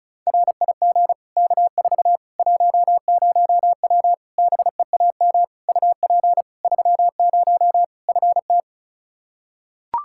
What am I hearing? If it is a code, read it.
Morse code, heard as a keyed tone, about 36 wpm, 700 Hz. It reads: RIG K4 10W BEAM UP 30 FT
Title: K1USN SST QSO Practice 35wpm